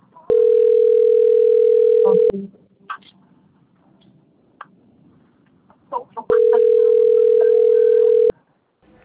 Ringback tone (North America), 440 + 480 Hz.
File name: rg-1600-unknown- - - .64498